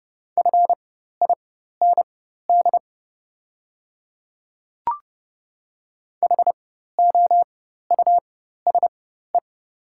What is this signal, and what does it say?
Morse code, heard as a keyed tone, about 30 wpm, spaced as for 14 wpm, 700 Hz. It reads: FIND HOUSE
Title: 4x Character Spacing - Top 200 Words - 30wpm